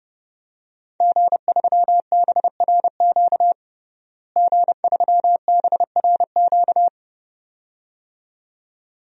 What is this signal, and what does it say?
Morse code, heard as a keyed tone, about 32 wpm, 700 Hz. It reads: G3BRQ G3BRQ
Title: DXCC-100 Call Signs - 5 Characters - Repeated 2x - 30wpm